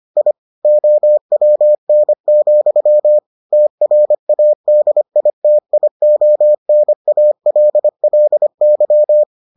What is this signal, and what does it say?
Morse code, heard as a keyed tone, about 25 wpm, 600 Hz. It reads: I OWN, TRADITIONALLY